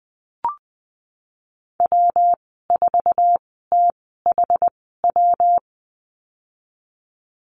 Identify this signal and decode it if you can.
Morse code, heard as a keyed tone, about 20 wpm, spaced as for 14 wpm, 700 Hz. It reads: W4THW
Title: 128 Lesson 21g - Number 6 - Call Signs 20wpm